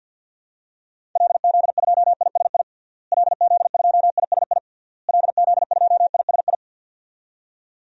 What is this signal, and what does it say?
Morse code, heard as a keyed tone, about 52 wpm, 700 Hz. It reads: LZ2ISI LZ2ISI LZ2ISI
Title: DXCC-100 Call Signs - 6 Characters - Repeated 3x - 50wpm